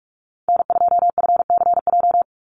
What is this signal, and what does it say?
Morse code, heard as a keyed tone, about 44 wpm, 700 Hz. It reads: N2FCJ
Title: USA Call Signs 45wpm